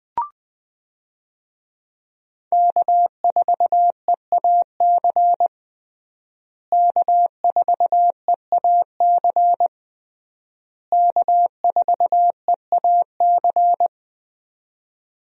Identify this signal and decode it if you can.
Morse code, heard as a keyed tone, about 20 wpm, 700 Hz. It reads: K4EAC K4EAC K4EAC
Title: USA Call Signs - 5 Characters - Repeated 3x - 20wpm